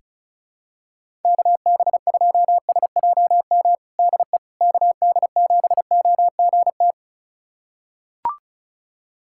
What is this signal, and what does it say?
Morse code, heard as a keyed tone, about 36 wpm, 700 Hz. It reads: KB2SJM DE KD7OGT
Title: USA Callsign DE Callsign - 6 Characters - 35wpm